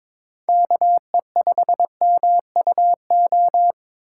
Morse code, heard as a keyed tone, about 22 wpm, 700 Hz. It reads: KE5MUO
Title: USA Call Signs - 6 Characters - 2x Character Spacing - 22wpm